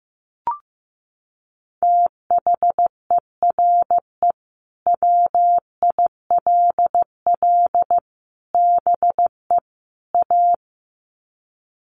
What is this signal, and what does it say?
Morse code, heard as a keyed tone, about 15 wpm, 700 Hz. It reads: THERE WILL BE A
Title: Sets of 4 Words 15wpm